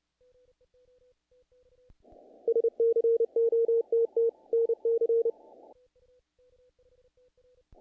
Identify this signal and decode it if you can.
Morse code, heard as a keyed tone, about 30 wpm, 500 Hz. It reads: SCOTT NC
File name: contests / naqp-cw-k5tr-op-k5pi- -1800